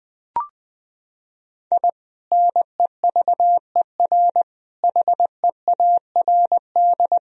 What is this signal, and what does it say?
Morse code, heard as a keyed tone, about 21 wpm, 700 Hz. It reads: I NEVER HEARD
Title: Sentences from Top 1000 Words - Review Part 1 - 20wpm